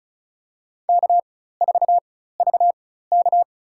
Morse code, heard as a keyed tone, about 36 wpm, spaced as for 16 wpm, 700 Hz. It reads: K4VK
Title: USA Call Signs - 4 Characters - 4x Character Spacing - 35wpm